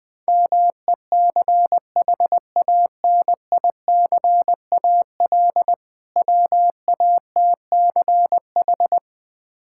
Morse code, heard as a keyed tone, about 20 wpm, 700 Hz. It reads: MECHANICAL WATCH
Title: Difficult Words - Speed-Racing with Hints - 20wpm